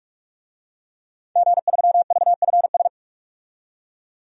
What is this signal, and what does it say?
Morse code, heard as a keyed tone, about 43 wpm, 700 Hz. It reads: G3VFS
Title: DXCC-100 Call Signs - 5 Characters - 45wpm